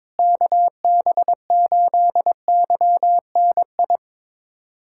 Morse code, heard as a keyed tone, about 22 wpm, 700 Hz. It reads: KB8YNI
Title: USA Callsign DE Callsign 22wpm